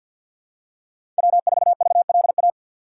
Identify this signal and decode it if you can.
Morse code, heard as a keyed tone, about 50 wpm, 700 Hz. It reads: W4VLA
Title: USA Call Signs - 5 Characters - 3x Character Spacing - 50wpm